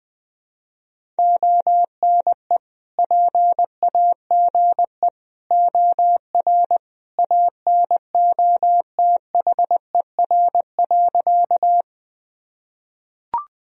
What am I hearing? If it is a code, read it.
Morse code, heard as a keyed tone, about 19 wpm, 700 Hz. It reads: ONE PAGE OR ANOTHER.
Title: Sentences from Top 1000 Words - Review Part 1 - 20wpm